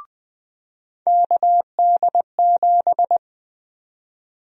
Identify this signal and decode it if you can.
Morse code, heard as a keyed tone, about 20 wpm, 700 Hz. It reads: KD7
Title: USA Call Sign Prefixes - 20wpm